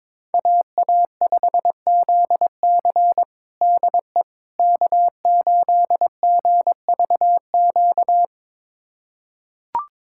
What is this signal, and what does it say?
Morse code, heard as a keyed tone, about 22 wpm, 700 Hz. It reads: AA5ZC DE K8GVQ
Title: USA Callsign DE Callsign - 5 Characters - 22wpm